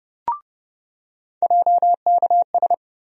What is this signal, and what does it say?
Morse code, heard as a keyed tone, about 30 wpm, 700 Hz. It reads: JKS
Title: USA Call Sign Postfixes - 3 Letters 30wpm